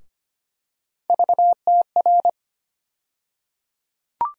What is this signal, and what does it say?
Morse code, heard as a keyed tone, about 25 wpm, 700 Hz. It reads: VTR
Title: USA Call Sign Postfixes - 3 Letters 25wpm